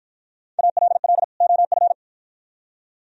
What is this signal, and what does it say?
Morse code, heard as a keyed tone, about 53 wpm, 700 Hz. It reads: ALL OF